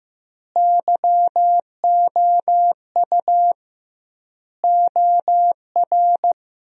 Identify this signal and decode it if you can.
Morse code, heard as a keyed tone, about 16 wpm, 700 Hz. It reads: YOU OR
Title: 2x Word-Spacing - Sentences from Top 200 Words - 15wpm